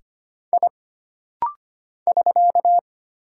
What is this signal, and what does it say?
Morse code, heard as a keyed tone, about 25 wpm, 700 Hz. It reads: I <SK>
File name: Single Character (Letter:Number:Punctuation:Prosign) - Rapid-Fire - 25wpm